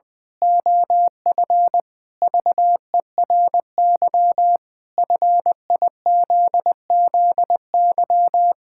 Morse code, heard as a keyed tone, about 20 wpm, 700 Hz. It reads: OF VERY FIZZY
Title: Pangram Challenge v2 - 20wpm